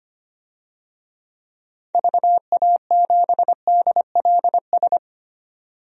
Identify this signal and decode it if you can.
Morse code, heard as a keyed tone, about 25 wpm, 700 Hz. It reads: VA7DLS
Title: DXCC-100 Call Signs - 6 Characters - 25wpm